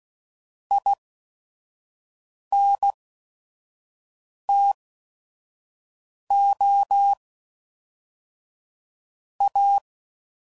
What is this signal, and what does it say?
Morse code, heard as a keyed tone, about 16 wpm, spaced as for 5 wpm, 800 Hz. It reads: INTO A
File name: CW Code Groups @5WPM / QSOs / kc7sac